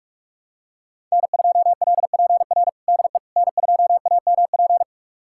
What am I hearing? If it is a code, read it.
Morse code, heard as a keyed tone, about 42 wpm, 700 Hz. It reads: N2LPR DE N2AMP
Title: USA Callsign DE Callsign - 5 Characters - 45wpm